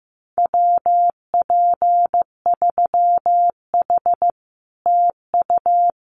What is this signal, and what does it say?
Morse code, heard as a keyed tone, about 15 wpm, 700 Hz. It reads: WP3H TU
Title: Common QSO Phrases Encore 15wpm